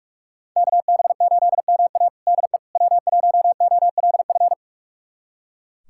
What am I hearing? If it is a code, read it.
Morse code, heard as a keyed tone, about 44 wpm, 700 Hz. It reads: KB8MA DE W1OLF
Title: USA Callsign DE Callsign - 5 Characters - 45wpm